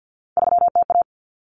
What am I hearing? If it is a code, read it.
Morse code, heard as a keyed tone, about 50 wpm, 700 Hz. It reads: 3TA
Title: Common Cut Numbers - 3-Digit 50wpm